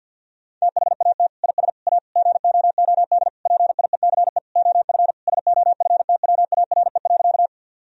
Morse code, heard as a keyed tone, about 50 wpm, 700 Hz. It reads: THAT IS A GOOD PIECE OF SOFTWARE.
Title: Sentences from Top 1000 Words - Review Part 3 - 50wpm